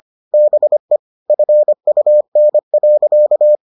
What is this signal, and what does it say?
Morse code, heard as a keyed tone, about 25 wpm, 600 Hz. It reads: BE FUN.